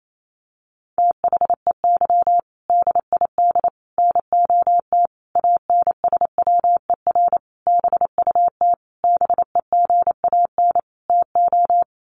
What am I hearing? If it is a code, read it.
Morse code, heard as a keyed tone, about 28 wpm, 700 Hz. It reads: THEY DID NOT ANSWER BUT BEGAN TO
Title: Sentences from Top 600 Words 28wpm